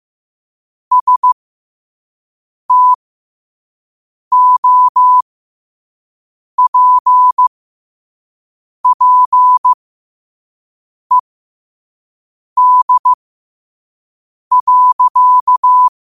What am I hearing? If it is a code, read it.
Morse code, heard as a keyed tone, about 15 wpm, spaced as for 5 wpm, 1000 Hz. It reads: STOPPED.
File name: qotdfiles / 5wpm